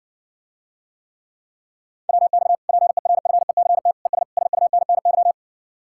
Morse code, heard as a keyed tone, about 60 wpm, 700 Hz. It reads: WX PERFECT ES SUNNY